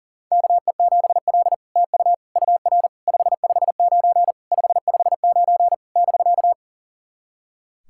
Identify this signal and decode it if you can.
Morse code, heard as a keyed tone, about 42 wpm, 700 Hz. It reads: KE7L TU UR 559 559 <BK>
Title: SOTA QSO Practice 40wpm